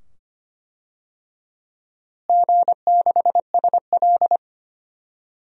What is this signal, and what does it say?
Morse code, heard as a keyed tone, about 25 wpm, 700 Hz. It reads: G6SL
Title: DXCC-100 Call Signs - 4 Characters - Repeated 2x - 25wpm